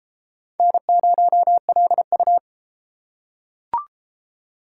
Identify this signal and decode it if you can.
Morse code, heard as a keyed tone, about 32 wpm, 700 Hz. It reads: N0LU